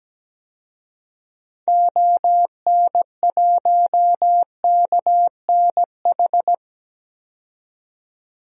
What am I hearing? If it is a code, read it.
Morse code, heard as a keyed tone, about 17 wpm, 700 Hz. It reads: ON1KNH
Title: DXCC-100 Call Signs - 6 Characters - 17wpm